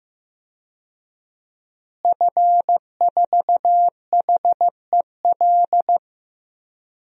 Morse code, heard as a keyed tone, about 15 wpm, 700 Hz. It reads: F4HEL
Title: DXCC-100 Call Signs - 5 Characters - Repeated 3x - 15wpm